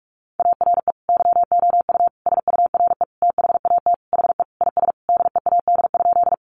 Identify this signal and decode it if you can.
Morse code, heard as a keyed tone, about 44 wpm, 700 Hz. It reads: ARE YOU SURE THAT HE IS DEAD?
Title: Sentences from Top 1500 Words 45wpm